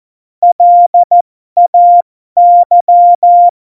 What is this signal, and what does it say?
Morse code, heard as a keyed tone, about 14 wpm, spaced as for 12 wpm, 700 Hz. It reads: LAY
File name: Morsekurs / Witze